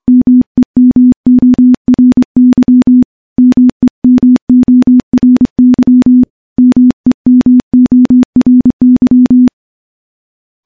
Morse code, heard as a keyed tone, about 24 wpm, 250 Hz. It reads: MEMORY MEMORY MEMORY
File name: morse-training-25WPM